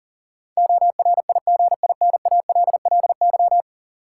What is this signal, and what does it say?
Morse code, heard as a keyed tone, about 40 wpm, 700 Hz. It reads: ORIGINALLY
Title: Sentences from Top 1500 Words - New Word Drill - 40wpm